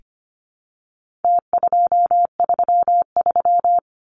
Morse code, heard as a keyed tone, about 25 wpm, 700 Hz. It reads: T233
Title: Numbers - Contest Style - 4-Digit 25wpm